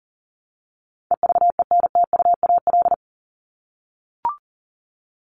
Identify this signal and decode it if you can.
Morse code, heard as a keyed tone, about 40 wpm, 700 Hz. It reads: EVENTUAL
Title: LY Suffix - Top 100 Words with Base Words 40wpm